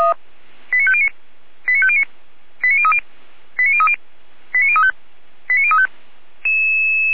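Five-tone selective call (ZVEI), heard as quick runs of five tones, tones of about 70 ms. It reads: 89509 89509 89030 89030 89035 89035
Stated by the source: ZVEI five-tone selective call 89010, received via SDR, recorded by Feuerwehr Orlamünde fire brigade.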